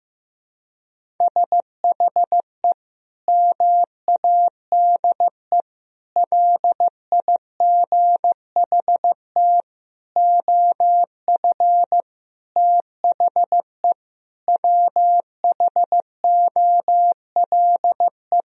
Morse code, heard as a keyed tone, about 15 wpm, 700 Hz. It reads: SHE MADE LIGHT OF THE WHOLE
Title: Sentences from Top 1000 Words - Review Part 1 - 15wpm